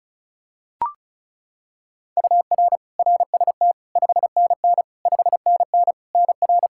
Morse code, heard as a keyed tone, about 34 wpm, 700 Hz. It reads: UR RST 5NN 5NN NR
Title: K1USN SST QSO Practice 35wpm